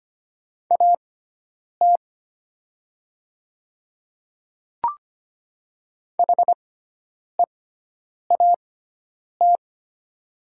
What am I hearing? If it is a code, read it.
Morse code, heard as a keyed tone, about 25 wpm, spaced as for 9 wpm, 700 Hz. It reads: AT HEAT
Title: 6x Character Spacing - Top 500 Words - 25wpm